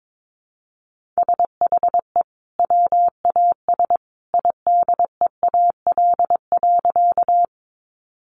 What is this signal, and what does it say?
Morse code, heard as a keyed tone, about 22 wpm, 700 Hz. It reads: SHE WAS IDEAL.
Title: Sentences from Top 1400 Words 22wpm